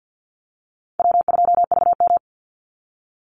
Morse code, heard as a keyed tone, about 50 wpm, 700 Hz. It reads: W2VM
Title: Easy Call Signs Repeated 3x - 50wpm